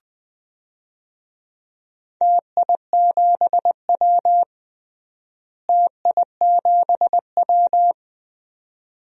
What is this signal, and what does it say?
Morse code, heard as a keyed tone, about 20 wpm, 700 Hz. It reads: TI7W TI7W
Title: DXCC-100 Call Signs - 4 Characters - Repeated 2x - 20wpm